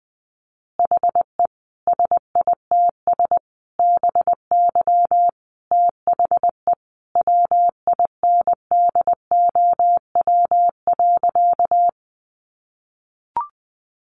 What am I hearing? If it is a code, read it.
Morse code, heard as a keyed tone, about 20 wpm, 700 Hz. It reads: HE SITS BY THE WINDOW.
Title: Additional Easy Phrases 20wpm